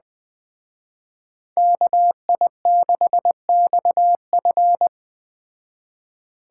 Morse code, heard as a keyed tone, about 20 wpm, 700 Hz. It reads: KI6XF